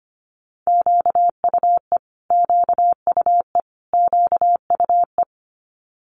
Morse code, heard as a keyed tone, about 25 wpm, 700 Hz. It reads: QUE QUE QUE
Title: Multi-letter Phonemes Repeated 3x 25wpm